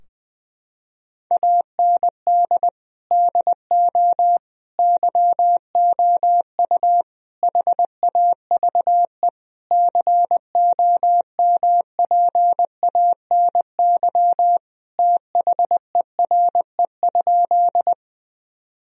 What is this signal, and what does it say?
Morse code, heard as a keyed tone, about 20 wpm, 700 Hz. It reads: AND DO YOU HAVE COMPANY THERE?
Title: Sentences from Top 1000 Words - Review Part 3 - 20wpm